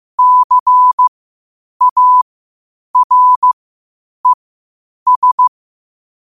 Morse code, heard as a keyed tone, about 15 wpm, spaced as for 8 wpm, 1000 Hz. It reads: CARES
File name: qotdfiles / 8wpm